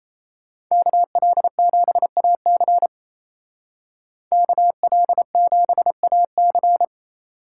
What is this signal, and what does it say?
Morse code, heard as a keyed tone, about 36 wpm, 700 Hz. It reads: KL7AC KL7AC